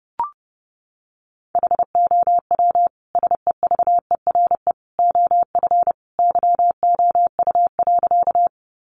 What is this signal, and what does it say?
Morse code, heard as a keyed tone, about 30 wpm, 700 Hz. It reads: HOW SEVERE OF YOU.
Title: Sentences from Top 1400 Words 30wpm